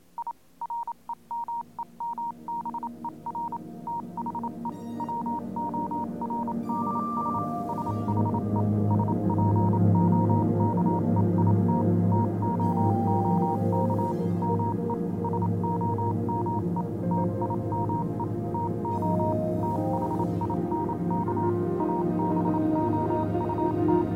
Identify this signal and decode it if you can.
Morse code, heard as a keyed tone, about 27 wpm, 940 Hz, with noise. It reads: I REMEMBER THE WORDS SHE SPOKE IT TOOK ME SOME TIME TO THINK ABOUT
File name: do you remember the song she sang (morse code version)